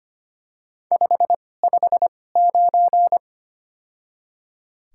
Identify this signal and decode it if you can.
Morse code, heard as a keyed tone, about 25 wpm, spaced as for 18 wpm, 700 Hz. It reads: 559
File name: Top 90 QSO Elements - 2x Character Spacing - 25wpm